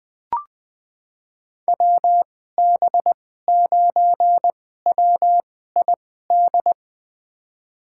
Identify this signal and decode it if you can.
Morse code, heard as a keyed tone, about 20 wpm, spaced as for 15 wpm, 700 Hz. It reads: WB9WID
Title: USA Call Signs - 6 Characters - 2x Character Spacing - 20wpm